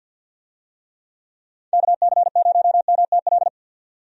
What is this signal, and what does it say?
Morse code, heard as a keyed tone, about 52 wpm, 700 Hz. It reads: KC0MTL
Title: USA Call Signs - 6 Characters - 50wpm